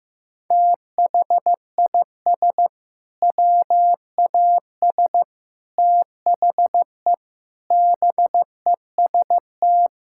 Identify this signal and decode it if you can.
Morse code, heard as a keyed tone, about 14 wpm, 700 Hz. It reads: THIS WAS THE BEST
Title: Sentences from Top 1600 Words 15wpm